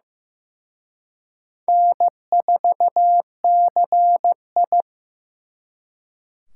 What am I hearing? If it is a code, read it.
Morse code, heard as a keyed tone, about 18 wpm, 700 Hz. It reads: N4CI
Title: Speed Racing - USA 1x2 Call Signs 15wpm